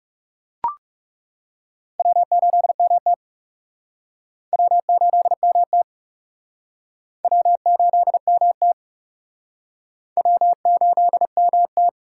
Morse code, heard as a keyed tone, about 39 wpm, 700 Hz. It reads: W8MT W8MT W8MT W8MT